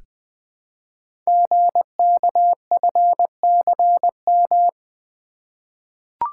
Morse code, heard as a keyed tone, about 20 wpm, 700 Hz. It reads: GKFCM
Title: Code Groups - 5 Letters - 20wpm